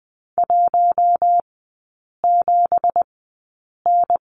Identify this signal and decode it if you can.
Morse code, heard as a keyed tone, about 20 wpm, spaced as for 8 wpm, 690 Hz. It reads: 17N
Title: 205 Lesson 33e - Letter j - Three Characters 20wpm